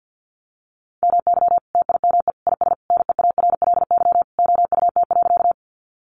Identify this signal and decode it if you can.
Morse code, heard as a keyed tone, about 50 wpm, 700 Hz. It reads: MY TIME IS NEARLY OUT.
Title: Sentences from Top 1300 Words 50wpm